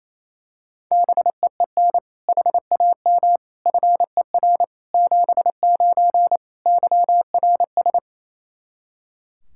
Morse code, heard as a keyed tone, about 28 wpm, 700 Hz. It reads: BEEN HAM FER 79 YRS
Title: Practice QSO - Exchange 3 - More Info - 28wpm